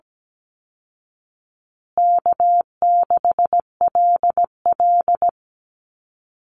Morse code, heard as a keyed tone, about 17 wpm, 700 Hz. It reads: K6LL
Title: Call Signs Repeated 3x - 17wpm